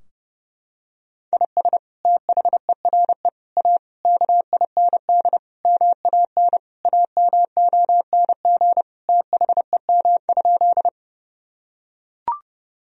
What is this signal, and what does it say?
Morse code, heard as a keyed tone, about 30 wpm, 700 Hz. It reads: IS THERE A KIND MAN AMONG THEM?
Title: Sentences from Top 1000 Words - Review Part 3 - 30wpm